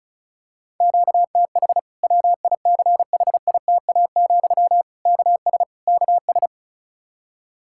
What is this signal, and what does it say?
Morse code, heard as a keyed tone, about 34 wpm, 700 Hz. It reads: QTH WICHITA, KS KS
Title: K1USN SST QSO Practice 35wpm